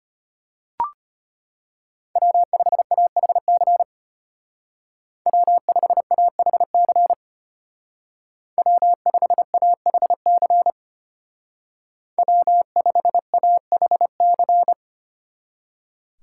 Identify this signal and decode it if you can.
Morse code, heard as a keyed tone, about 33 wpm, 700 Hz. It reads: W5AHC W5AHC W5AHC W5AHC